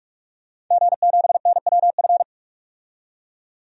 Morse code, heard as a keyed tone, about 44 wpm, 690 Hz. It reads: G7NWF
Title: DXCC-100 Call Signs - 5 Characters - 45wpm